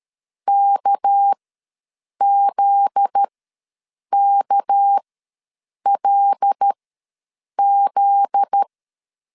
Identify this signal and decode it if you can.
Morse code, heard as a keyed tone, about 13 wpm, spaced as for 7 wpm, 800 Hz. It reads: KZKLZ
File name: LEZIONI / 7 GF9Z / 7 GF9Z ws 12wpm 3321